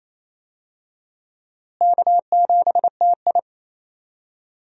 Morse code, heard as a keyed tone, about 28 wpm, 700 Hz. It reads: K7TI
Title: USA Call Signs - 4 Characters - Repeated 3x - 28wpm